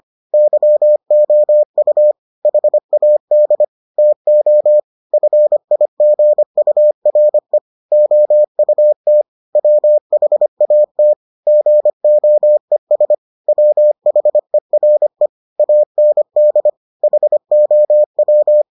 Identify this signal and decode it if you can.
Morse code, heard as a keyed tone, about 25 wpm, 600 Hz. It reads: YOU HAD TO FIGURE OUT WHAT GOES WHERE AND HOW